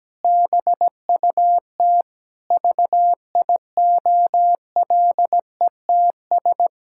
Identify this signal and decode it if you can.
Morse code, heard as a keyed tone, about 17 wpm, 700 Hz. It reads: BUT VIOLETS
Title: Pangram Challenge v2 - 17wpm